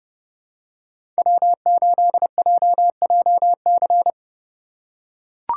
Morse code, heard as a keyed tone, about 31 wpm, 700 Hz. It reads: W8JJC
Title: USA Call Signs - 3x Character Spacing - 30wpm